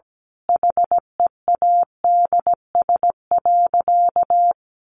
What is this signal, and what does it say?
Morse code, heard as a keyed tone, about 17 wpm, 700 Hz. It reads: HEADS.
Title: A Sound Of Thunder 17wpm